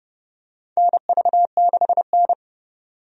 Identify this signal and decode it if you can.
Morse code, heard as a keyed tone, about 30 wpm, 700 Hz. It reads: NV6N
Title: USA Call Signs - 4 Characters - 30wpm